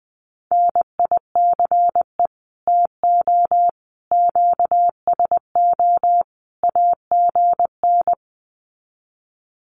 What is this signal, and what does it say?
Morse code, heard as a keyed tone, about 19 wpm, 700 Hz. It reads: NICE TO QSO AGN
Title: Common QSO Phrases 20wpm